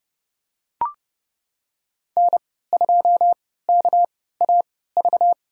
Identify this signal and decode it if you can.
Morse code, heard as a keyed tone, about 30 wpm, spaced as for 17 wpm, 700 Hz. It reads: N2KAV
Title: USA Call Signs - 5 Characters - 3x Character Spacing - 30wpm